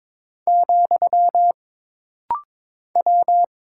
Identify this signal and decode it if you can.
Morse code, heard as a keyed tone, about 22 wpm, 700 Hz. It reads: , W
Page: This rapid-fire timing is geared to push your ICR (Instant Character Recognition) skills.